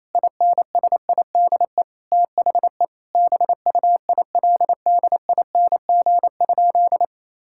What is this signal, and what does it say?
Morse code, heard as a keyed tone, about 28 wpm, 700 Hz. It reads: INSIDE THE BUILDING?
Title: Sentences from Top 1000 Words - Review Part 6 - 28wpm